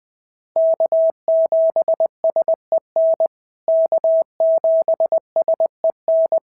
Morse code, heard as a keyed tone, about 20 wpm, 650 Hz. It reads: K7SEN K7SEN
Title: POTA QSO Practice - 20wpm